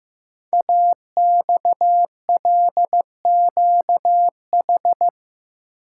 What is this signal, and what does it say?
Morse code, heard as a keyed tone, about 15 wpm, 700 Hz. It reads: AXLQH
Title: Code Groups - 5 Letters - 15wpm